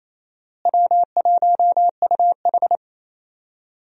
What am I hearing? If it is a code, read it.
Morse code, heard as a keyed tone, about 28 wpm, 700 Hz. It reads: W1UH
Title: USA Call Signs - 4 Characters - Repeated 2x - 28wpm